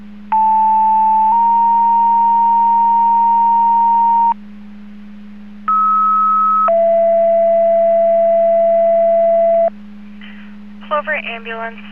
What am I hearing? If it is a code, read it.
Two-tone page, heard as a one-second tone, then a three-second tone, about 880 then 930 Hz, about 1290 then 670 Hz, with speech. It is heard from 0.3 to 9.7 s.